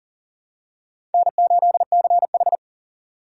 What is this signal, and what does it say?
Morse code, heard as a keyed tone, about 38 wpm, 700 Hz. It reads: N8CH
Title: USA Call Signs - 4 Characters - 4x Character Spacing - 40wpm